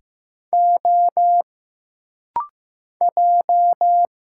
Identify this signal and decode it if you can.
Morse code, heard as a keyed tone, about 15 wpm, 700 Hz. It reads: O J
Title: Single Letter-Number Alt - Rapid-Fire 15wpm